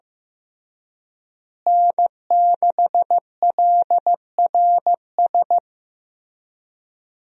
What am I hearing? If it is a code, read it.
Morse code, heard as a keyed tone, about 15 wpm, 700 Hz. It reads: N6LRS
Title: USA Call Signs - 5 Characters - Repeated 3x - 15wpm